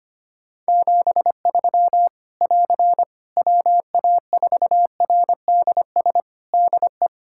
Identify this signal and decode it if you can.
Morse code, heard as a keyed tone, about 25 wpm, 700 Hz. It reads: 73 <AR> WA4RDS DE
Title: Practice QSO - Exchange 4 - Ending - 25wpm